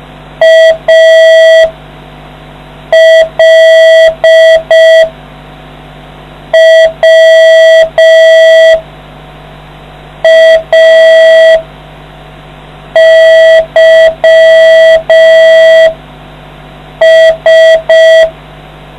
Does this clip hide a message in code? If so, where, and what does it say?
Yes: Morse code from 0.4 to 18.3 s, heard as a keyed tone, about 5 wpm, spaced as for 4 wpm, 650 Hz. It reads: ALWAYS